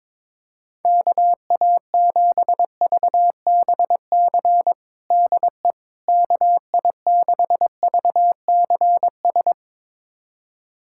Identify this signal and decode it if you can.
Morse code, heard as a keyed tone, about 22 wpm, 700 Hz. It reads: KA7VBC DE KI6VCS
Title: USA Callsign DE Callsign - 6 Characters - 22wpm